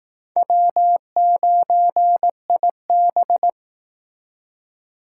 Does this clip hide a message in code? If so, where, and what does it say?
Yes: Morse code from 0.4 to 3.5 s, heard as a keyed tone, about 19 wpm, 700 Hz. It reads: W9IB